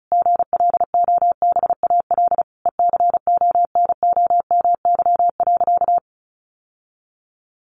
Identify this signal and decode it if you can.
Morse code, heard as a keyed tone, about 34 wpm, 700 Hz. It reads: GLOBAL ECONOMY.
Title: Sentences from Top 1200 Words 35wpm